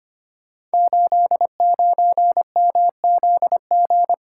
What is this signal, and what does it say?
Morse code, heard as a keyed tone, about 25 wpm, 700 Hz. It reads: 89MZG
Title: Code Groups - 5 Letter:Numbers - 25wpm